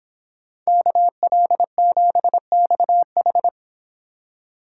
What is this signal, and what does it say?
Morse code, heard as a keyed tone, about 29 wpm, 700 Hz. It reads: KL7XH